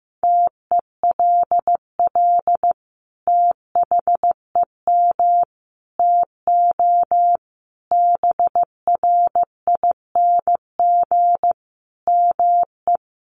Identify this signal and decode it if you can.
Morse code, heard as a keyed tone, about 15 wpm, 700 Hz. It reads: TELL THEM TO BRING ME
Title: Sentences from Top 1200 Words 15wpm